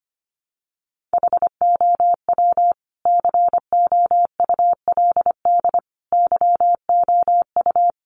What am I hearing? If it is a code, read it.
Morse code, heard as a keyed tone, about 25 wpm, 700 Hz. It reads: HOW COULD YOU
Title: Sentences from Top 1500 Words 25wpm